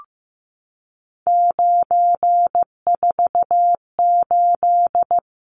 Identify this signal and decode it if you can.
Morse code, heard as a keyed tone, about 15 wpm, 700 Hz. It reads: 948
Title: Numbers - 3-Digit 15wpm